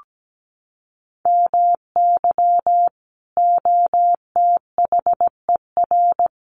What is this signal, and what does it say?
Morse code, heard as a keyed tone, about 17 wpm, 700 Hz. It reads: MY OTHER
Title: Sentences from Top 1300 Words 17wpm